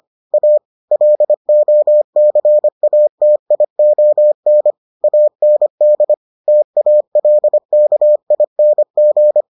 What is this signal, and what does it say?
Morse code, heard as a keyed tone, about 25 wpm, 600 Hz. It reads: A LOCATION AND TALKING